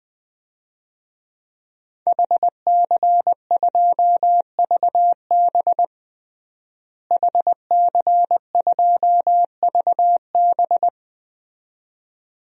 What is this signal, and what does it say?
Morse code, heard as a keyed tone, about 20 wpm, 700 Hz. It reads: HC2VB HC2VB
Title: DXCC-100 Call Signs - 5 Characters - Repeated 2x - 20wpm